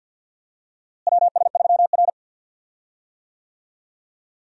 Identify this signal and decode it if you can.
Morse code, heard as a keyed tone, about 52 wpm, 700 Hz. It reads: WS3R
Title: USA Call Signs - 4 Characters - Repeated 3x - 50wpm